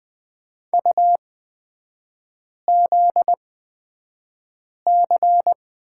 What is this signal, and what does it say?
Morse code, heard as a keyed tone, about 20 wpm, spaced as for 5 wpm, 700 Hz. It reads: UZC
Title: 243 Lesson 39e - Letter z - Three Characters 20wpm